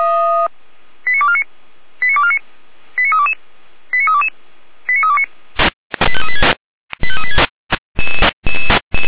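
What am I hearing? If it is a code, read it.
Five-tone selective call (ZVEI), heard as quick runs of five tones, tones of about 70 ms. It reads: 89249 89249 89322 89322 89329 85386 85386
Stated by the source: Five-tone ZVEI selective call received on an SDR by feuerwehr-orlamuende (fire brigade Orlamünde).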